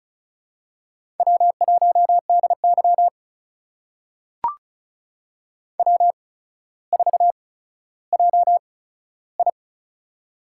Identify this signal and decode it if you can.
Morse code, heard as a keyed tone, about 36 wpm, spaced as for 15 wpm, 700 Hz. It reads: W1DY W4JI
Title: USA Call Signs - 4 Characters - 8x Character Spacing - 35wpm